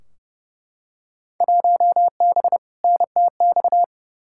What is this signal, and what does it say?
Morse code, heard as a keyed tone, about 32 wpm, 700 Hz. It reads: 1B NTX
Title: Field Day Exchanges - 30wpm